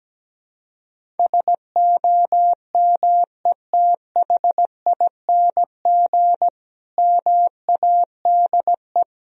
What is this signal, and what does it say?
Morse code, heard as a keyed tone, about 17 wpm, 700 Hz. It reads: SOMETHING MADE
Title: Sentences from Top 1000 Words - Review Part 1 - 17wpm